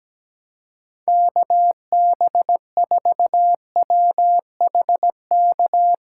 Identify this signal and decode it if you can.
Morse code, heard as a keyed tone, about 17 wpm, 700 Hz. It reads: KB4WHK
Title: USA Call Signs - 6 Characters - 6x Character Spacing - 17wpm